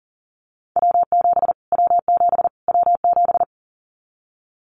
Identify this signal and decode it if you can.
Morse code, heard as a keyed tone, about 37 wpm, 700 Hz. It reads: W7 W7 W7
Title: Callsign Prefixes for Top 25 DXCC - Repeated 3x - 40wpm